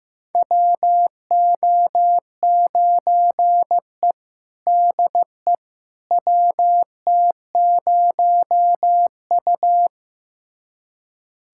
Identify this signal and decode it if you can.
Morse code, heard as a keyed tone, about 15 wpm, 700 Hz. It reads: WO9E DE WT0U
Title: USA Callsign DE Callsign - 4 Characters - 15wpm